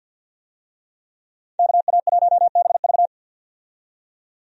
Morse code, heard as a keyed tone, about 50 wpm, 700 Hz. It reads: KA1BV
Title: USA Call Signs - 5 Characters - 50wpm